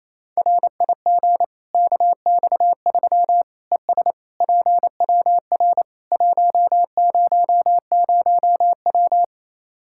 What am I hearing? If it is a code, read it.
Morse code, heard as a keyed tone, about 28 wpm, 700 Hz. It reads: RIG KX3 ES PWR 100W
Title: K1USN SST QSO Practice 28wpm